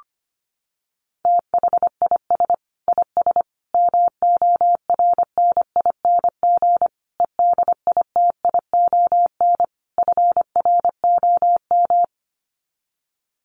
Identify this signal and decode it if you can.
Morse code, heard as a keyed tone, about 25 wpm, 700 Hz. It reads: THIS IS MORNING EDITION FROM
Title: Sets of 5 Words Encore2 25wpm